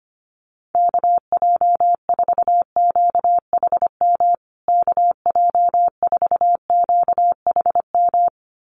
Morse code, heard as a keyed tone, about 25 wpm, 700 Hz. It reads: KJ4QHM KJ4QHM
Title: USA Call Signs Repeated 2x 25wpm